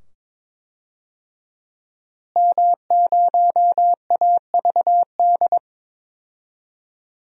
Morse code, heard as a keyed tone, about 21 wpm, 700 Hz. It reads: M0AVD
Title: DXCC-100 Call Signs - 5 Characters - Repeated 2x - 22wpm